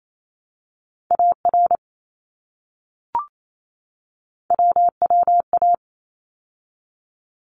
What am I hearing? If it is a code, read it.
Morse code, heard as a keyed tone, about 28 wpm, 700 Hz. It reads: AR WWA
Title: ARRL Sections 28wpm